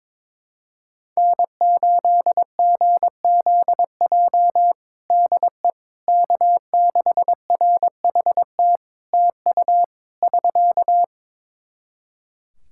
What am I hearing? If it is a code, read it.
Morse code, heard as a keyed tone, about 22 wpm, 700 Hz. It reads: N8GZJ DE K6RHT TU <SK>
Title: Practice QSO - Exchange 4 - Ending - 22wpm